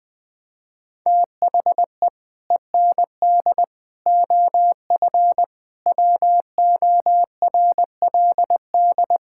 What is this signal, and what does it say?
Morse code, heard as a keyed tone, about 20 wpm, 700 Hz. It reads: THE END OF WORLD
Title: Sets of 6 Words - 20wpm